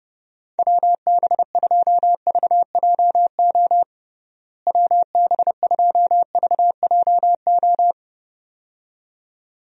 Morse code, heard as a keyed tone, about 32 wpm, 700 Hz. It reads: WB2VJO WB2VJO
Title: USA Call Signs - 6 Characters - Repeated 2x - 30wpm